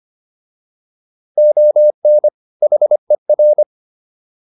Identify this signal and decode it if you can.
Morse code, heard as a keyed tone, about 25 wpm, 600 Hz. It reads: ON HER
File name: WD104 25